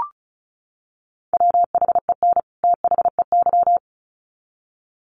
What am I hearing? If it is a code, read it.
Morse code, heard as a keyed tone, about 34 wpm, 700 Hz. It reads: WHEN THEY
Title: Sets of 2 Words 35wpm